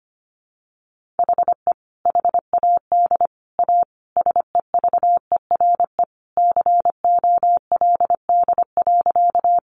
Morse code, heard as a keyed tone, about 25 wpm, 700 Hz. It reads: HE HAD A SEVERE COLD.
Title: Sentences from Top 1400 Words 25wpm